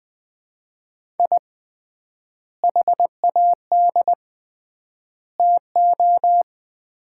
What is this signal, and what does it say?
Morse code, heard as a keyed tone, about 21 wpm, 700 Hz. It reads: I HAD TO
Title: 3x Word-Spacing - Sets of 3 Words - 20wpm